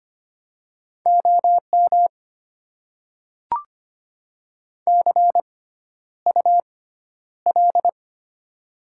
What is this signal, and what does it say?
Morse code, heard as a keyed tone, about 25 wpm, spaced as for 11 wpm, 700 Hz. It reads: OM CUL
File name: Top 90 QSO Elements - 6x Character Spacing - 25wpm